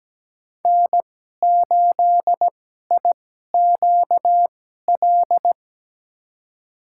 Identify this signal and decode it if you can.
Morse code, heard as a keyed tone, about 17 wpm, spaced as for 12 wpm, 700 Hz. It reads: N8IQL
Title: USA Call Signs - 5 Characters - 2x Character Spacing - 17wpm